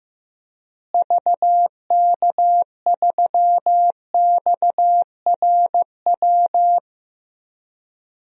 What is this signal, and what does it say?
Morse code, heard as a keyed tone, about 15 wpm, 700 Hz. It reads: VK3XRW
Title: DXCC-100 Call Signs - 6 Characters - Repeated 2x - 15wpm